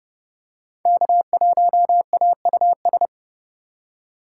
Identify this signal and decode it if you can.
Morse code, heard as a keyed tone, about 28 wpm, 700 Hz. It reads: K1AUS